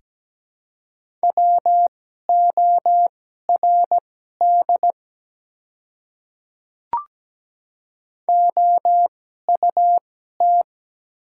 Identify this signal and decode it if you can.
Morse code, heard as a keyed tone, about 17 wpm, spaced as for 12 wpm, 700 Hz. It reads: WORD OUT
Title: 2x Character Spacing - Top 100 Words - 17wpm